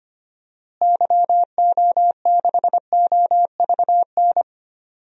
Morse code, heard as a keyed tone, about 25 wpm, 700 Hz. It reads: YO6OVN
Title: DXCC-100 Call Signs - 6 Characters - Repeated 2x - 25wpm